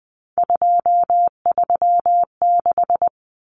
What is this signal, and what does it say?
Morse code, heard as a keyed tone, about 20 wpm, 700 Hz. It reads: 236
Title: Numbers - 3-Digit 20wpm